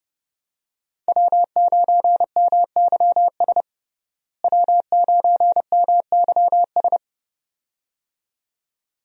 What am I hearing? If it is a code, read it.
Morse code, heard as a keyed tone, about 30 wpm, 700 Hz. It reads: W9MYS W9MYS
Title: USA Call Signs - 5 Characters - Repeated 2x - 30wpm